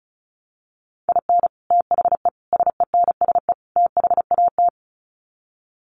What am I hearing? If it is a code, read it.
Morse code, heard as a keyed tone, about 34 wpm, 700 Hz. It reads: IN THE SENSE THAT
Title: Sets of 4 Words 35wpm